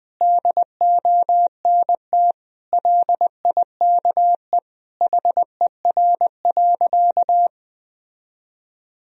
Morse code, heard as a keyed tone, about 20 wpm, 700 Hz. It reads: DONT LIKE HER.
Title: Sentences from Top 1000 Words - Review Part 2 - 20wpm